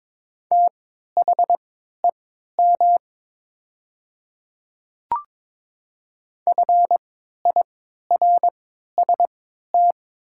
Morse code, heard as a keyed tone, about 22 wpm, spaced as for 12 wpm, 700 Hz. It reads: THEM FIRST
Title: 3x Character Spacing - Top 100 Words - 22wpm